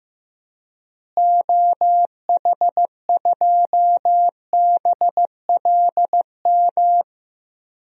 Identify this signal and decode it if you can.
Morse code, heard as a keyed tone, about 15 wpm, 700 Hz. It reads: OH2BLM
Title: DXCC-100 Call Signs - 6 Characters - Repeated 2x - 15wpm